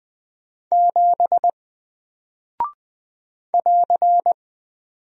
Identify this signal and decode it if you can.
Morse code, heard as a keyed tone, about 20 wpm, 700 Hz. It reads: 7 <AR>
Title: Single Character (Letter:Number:Punctuation:Prosign) Alt 20wpm